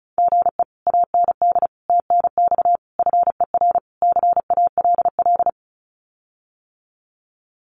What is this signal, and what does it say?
Morse code, heard as a keyed tone, about 34 wpm, 700 Hz. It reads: GE AND TNX FER CALL
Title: Common QSO Phrases Encore 35wpm